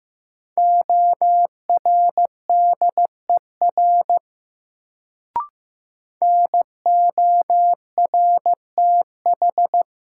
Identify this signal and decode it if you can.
Morse code, heard as a keyed tone, about 15 wpm, 700 Hz. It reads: ORDER NORTH
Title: Sentences from Top 400 Words - New Word Drill - 15wpm